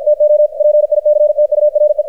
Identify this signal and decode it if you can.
Morse code, heard as a keyed tone, about 53 wpm, 600 Hz, with radio noise. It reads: TO PICTURE
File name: archived / CRSnet